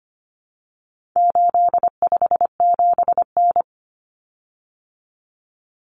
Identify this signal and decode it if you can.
Morse code, heard as a keyed tone, about 25 wpm, 700 Hz. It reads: 857N
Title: Common Cut Numbers - 4-Digit 25wpm